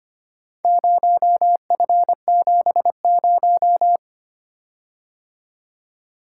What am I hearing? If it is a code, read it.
Morse code, heard as a keyed tone, about 25 wpm, 700 Hz. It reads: 0F70
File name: practice / 25wpm / Four Letter-Number Combinations - 25wpm